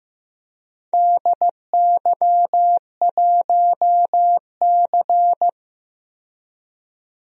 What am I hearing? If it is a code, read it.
Morse code, heard as a keyed tone, about 15 wpm, 700 Hz. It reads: DY1C
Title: DXCC-100 Call Signs - 4 Characters - 15wpm